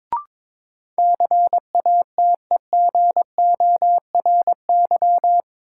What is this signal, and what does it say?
Morse code, heard as a keyed tone, about 22 wpm, 700 Hz. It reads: CATEGORY
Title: Sentences from Top 1000 Words - Review Part 7 - New Word Drill - 22wpm